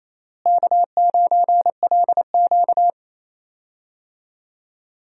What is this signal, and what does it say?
Morse code, heard as a keyed tone, about 28 wpm, 700 Hz. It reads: K9LQ
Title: USA Call Signs - 4 Characters - Repeated 3x - 28wpm